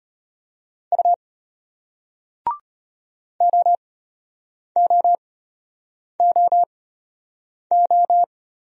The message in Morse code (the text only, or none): U OOOO